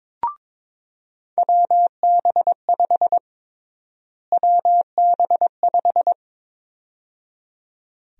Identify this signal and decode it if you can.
Morse code, heard as a keyed tone, about 21 wpm, 700 Hz. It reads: WB5 WB5
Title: USA Call Sign Prefixes - Repeated 2x - 22wpm